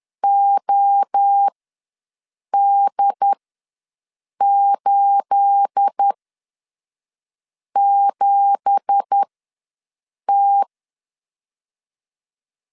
Morse code, heard as a keyed tone, about 10 wpm, spaced as for 5 wpm, 800 Hz. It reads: OD87T
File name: LEZIONI / 10 SIGNS / 10C DOT ws 10wpm 4213